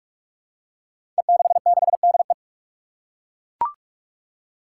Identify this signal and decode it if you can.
Morse code, heard as a keyed tone, about 42 wpm, 700 Hz. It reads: E66DE
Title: Five Challenging Letter-Number Combinations 45wpm